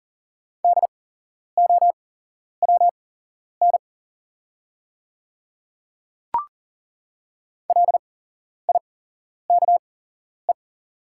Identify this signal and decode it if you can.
Morse code, heard as a keyed tone, about 40 wpm, spaced as for 11 wpm, 690 Hz. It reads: DOWN LIKE
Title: 8x Character Spacing - Top 200 Words - 40wpm